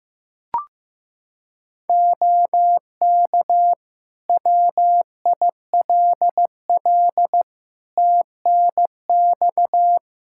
Morse code, heard as a keyed tone, about 15 wpm, 700 Hz. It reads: OK WILL TNX
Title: Practice QSO - Exchange 4 - Ending - 15wpm